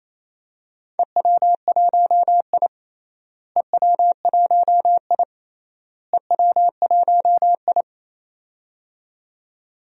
Morse code, heard as a keyed tone, about 28 wpm, 700 Hz. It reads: EW1I EW1I EW1I
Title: DXCC-100 Call Signs - 4 Characters - Repeated 3x - 28wpm